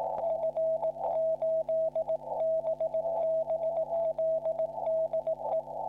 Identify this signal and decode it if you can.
Morse code, heard as a keyed tone, about 18 wpm, 640 Hz, with radio noise. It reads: F8BBL DE
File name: f5lmj / Hb0 / HB0 F5LMJ P QSO F8BBL 40m cw 20h 10 UTC 22 07 09